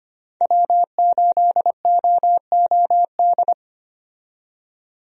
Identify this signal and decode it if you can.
Morse code, heard as a keyed tone, about 25 wpm, 700 Hz. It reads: W8OOD
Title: USA Call Signs - 5 Characters - 25wpm